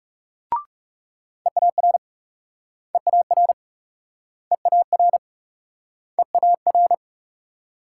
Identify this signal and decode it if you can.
Morse code, heard as a keyed tone, about 36 wpm, 700 Hz. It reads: EAR EAR EAR EAR
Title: Speed-Racing Multi-letter Phonemes - 30wpm